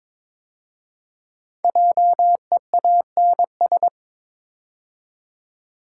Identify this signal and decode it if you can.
Morse code, heard as a keyed tone, about 22 wpm, 700 Hz. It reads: JEANS
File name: 5 Letter Words - First 3 Letters Given - 22wpm